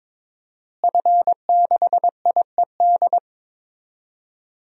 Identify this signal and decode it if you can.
Morse code, heard as a keyed tone, about 22 wpm, 700 Hz. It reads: F6IED
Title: DXCC-100 Call Signs - 5 Characters - Repeated 2x - 22wpm